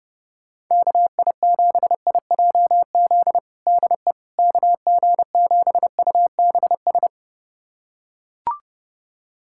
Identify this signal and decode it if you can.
Morse code, heard as a keyed tone, about 30 wpm, 700 Hz. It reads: KI7IJZ DE KG7UBS
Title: USA Callsign DE Callsign 30wpm